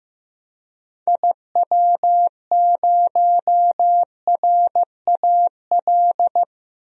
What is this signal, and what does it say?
Morse code, heard as a keyed tone, about 16 wpm, 700 Hz. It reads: IW0RAL
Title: DXCC-100 Call Signs - 6 Characters - 15wpm